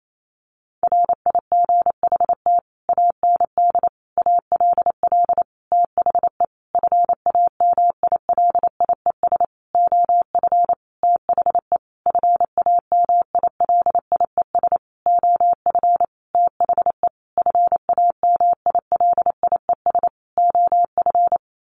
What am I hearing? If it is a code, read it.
Morse code, heard as a keyed tone, about 28 wpm, 700 Hz. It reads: RIGHT AND ALL THE FAMILIES OF THE FAMILIES OF THE FAMILIES OF
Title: A Sound Of Thunder 28wpm